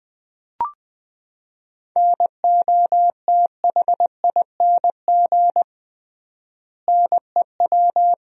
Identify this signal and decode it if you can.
Morse code, heard as a keyed tone, about 20 wpm, 700 Hz. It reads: NOTHING NEW
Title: 3x Word-Spacing - Sentences from Top 500 Words - 20wpm